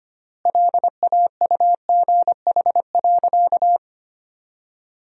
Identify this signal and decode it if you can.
Morse code, heard as a keyed tone, about 25 wpm, 700 Hz. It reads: LAUGH.
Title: 3x Word-Spacing - Sentences from Top 500 Words - 25wpm